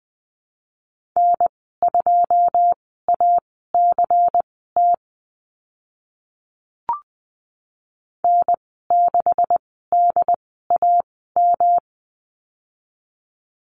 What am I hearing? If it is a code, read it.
Morse code, heard as a keyed tone, about 20 wpm, spaced as for 15 wpm, 700 Hz. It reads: N2ACT N6DAM
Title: 140 Lesson 23g - Letter f - Call Signs 20wpm